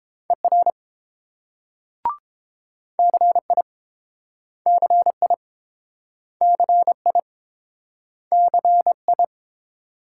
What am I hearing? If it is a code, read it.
Morse code, heard as a keyed tone, about 31 wpm, 700 Hz. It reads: ER CI CI CI CI